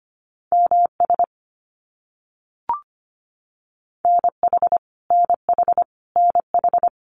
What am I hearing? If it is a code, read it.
Morse code, heard as a keyed tone, about 25 wpm, 700 Hz. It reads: MS NH NH NH
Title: US States Repeated 3x 25wpm